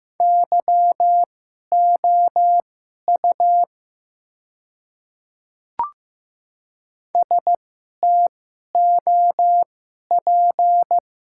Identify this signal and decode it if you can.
Morse code, heard as a keyed tone, about 15 wpm, spaced as for 11 wpm, 690 Hz. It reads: YOU STOP
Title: 2x Character Spacing - Top 300 Words - 15wpm